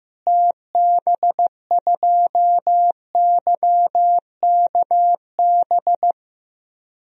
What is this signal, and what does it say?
Morse code, heard as a keyed tone, about 15 wpm, 700 Hz. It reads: TB2YKB
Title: DXCC-100 Call Signs - 6 Characters - Repeated 2x - 15wpm